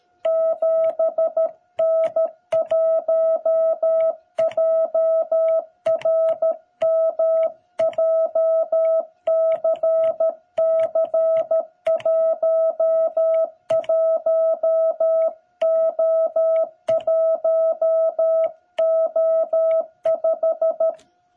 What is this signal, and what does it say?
Morse code, heard as a keyed tone, about 13 wpm, 660 Hz. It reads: 7N1JRM JCC11O1O5
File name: jh8cbh / 2021cw